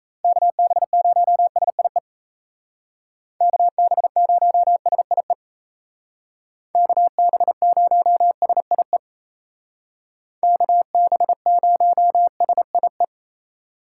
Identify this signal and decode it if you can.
Morse code, heard as a keyed tone, about 36 wpm, 700 Hz. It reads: KB0SIE KB0SIE KB0SIE KB0SIE